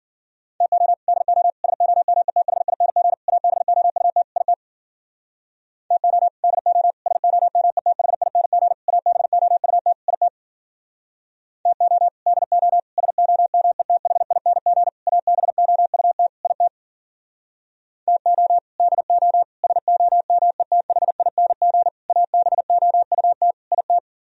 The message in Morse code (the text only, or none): TO DO SOMETHING ABOUT IT TO DO SOMETHING ABOUT IT TO DO SOMETHING ABOUT IT TO DO SOMETHING ABOUT IT